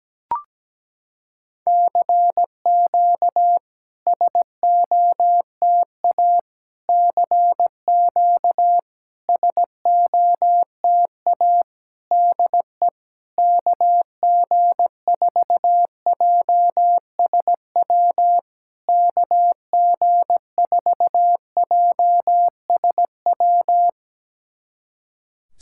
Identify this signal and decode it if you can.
Morse code, heard as a keyed tone, about 17 wpm, 700 Hz. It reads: CQ SOTA CQ SOTA DE KG4JSW KG4JSW
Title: SOTA QSO Practice 17wpm